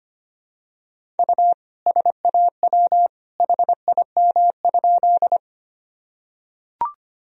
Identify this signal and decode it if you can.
Morse code, heard as a keyed tone, about 25 wpm, 700 Hz. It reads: U SAW HIM?
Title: Sentences from Top 300 Words and Abbreviations 25wpm